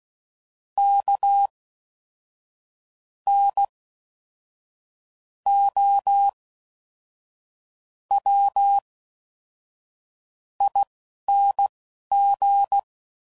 Morse code, heard as a keyed tone, about 16 wpm, spaced as for 5 wpm, 800 Hz. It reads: KNOWING